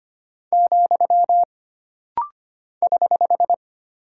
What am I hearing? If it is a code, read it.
Morse code, heard as a keyed tone, about 25 wpm, 700 Hz. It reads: , <HH>
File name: Single Punctuation-Prosign - Rapid-Fire - 25wpm